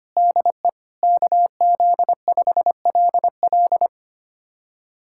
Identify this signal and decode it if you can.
Morse code, heard as a keyed tone, about 25 wpm, 700 Hz. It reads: DE KZ5LL
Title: USA Callsign DE Callsign 25wpm